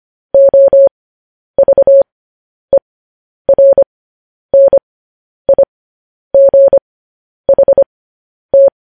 Morse code, heard as a keyed tone, about 25 wpm, spaced as for 10 wpm, 550 Hz. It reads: OVERNIGHT
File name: podcast / News headlines-1x-25x10